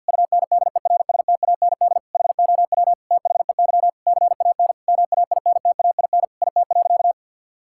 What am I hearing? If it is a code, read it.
Morse code, heard as a keyed tone, about 46 wpm, 700 Hz. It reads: UNDERSTAND HOW THEY CAN MAINTAIN IT.
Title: Sentences from Top 1000 Words - Review Part 5 - 50wpm